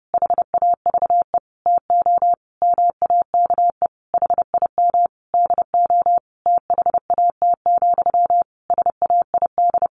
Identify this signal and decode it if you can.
Morse code, heard as a keyed tone, about 30 wpm, 700 Hz. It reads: HAVE TO MAKE HIM DO THAT, SAID
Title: A Sound Of Thunder 30wpm